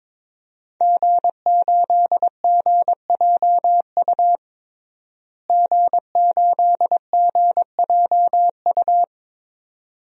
Morse code, heard as a keyed tone, about 22 wpm, 700 Hz. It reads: G8GJU G8GJU
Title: DXCC-100 Call Signs - 5 Characters - Repeated 2x - 22wpm